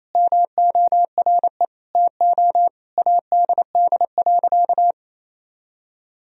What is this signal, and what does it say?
Morse code, heard as a keyed tone, about 28 wpm, 700 Hz. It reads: MORE TO ADD.
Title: Sentences from Top 1000 Words - Review Part 1 - 28wpm